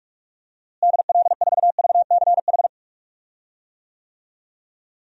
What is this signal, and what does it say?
Morse code, heard as a keyed tone, about 47 wpm, 700 Hz. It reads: DL4VCH
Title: DXCC-100 Call Signs - 6 Characters - Repeated 2x - 45wpm